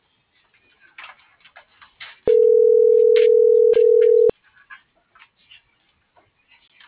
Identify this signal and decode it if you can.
Ringback tone (North America), 440 + 480 Hz.